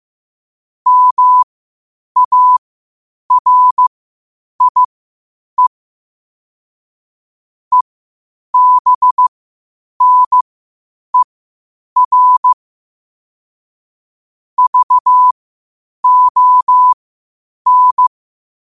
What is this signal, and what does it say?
Morse code, heard as a keyed tone, about 15 wpm, spaced as for 8 wpm, 1000 Hz. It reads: MARIE EBNER VON
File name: qotdfiles / 8wpm